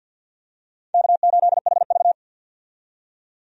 Morse code, heard as a keyed tone, about 52 wpm, 700 Hz. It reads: K8HV
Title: USA Call Signs - 4 Characters - 50wpm